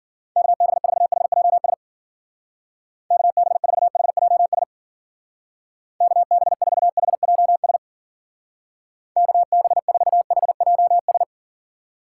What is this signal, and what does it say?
Morse code, heard as a keyed tone, about 52 wpm, 700 Hz. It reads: KB4HJS KB4HJS KB4HJS KB4HJS